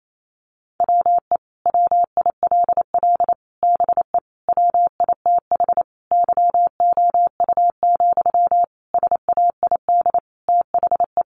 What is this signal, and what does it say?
Morse code, heard as a keyed tone, about 28 wpm, 700 Hz. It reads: WE WILL BE WITH YOU, SAID THE
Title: Sentences from Top 600 Words 28wpm